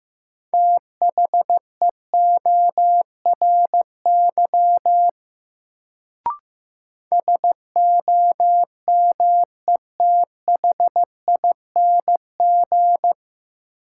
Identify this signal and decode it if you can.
Morse code, heard as a keyed tone, about 15 wpm, 700 Hz. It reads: THEORY SOMETHING
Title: Sentences from Top 600 Words - New Word Drill - 15wpm